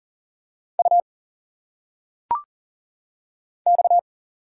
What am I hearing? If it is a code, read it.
Morse code, heard as a keyed tone, about 40 wpm, 700 Hz. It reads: U X
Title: Single Letters - Phonetic Pronunciation 40wpm